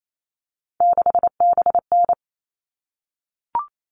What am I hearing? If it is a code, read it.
Morse code, heard as a keyed tone, about 28 wpm, 700 Hz. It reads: 6BN
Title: All Cut Numbers - 3-Digit 28wpm